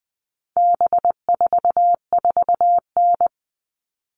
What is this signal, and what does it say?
Morse code, heard as a keyed tone, about 20 wpm, 700 Hz. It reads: B44N
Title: All Cut Numbers - 4-Digit 20wpm